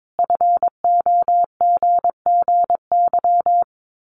Morse code, heard as a keyed tone, about 22 wpm, 700 Hz. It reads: FOGGY
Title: Common QSO Phrases 22wpm